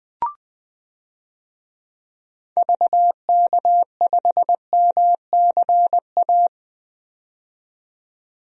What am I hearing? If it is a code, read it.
Morse code, heard as a keyed tone, about 21 wpm, 700 Hz. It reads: VK5MCA
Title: DXCC-100 Call Signs - 6 Characters - 20wpm